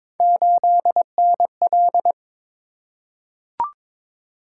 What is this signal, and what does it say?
Morse code, heard as a keyed tone, about 23 wpm, 700 Hz. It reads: 8NL
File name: practice / 22wpm / Three Letter-Number Combinations - 22wpm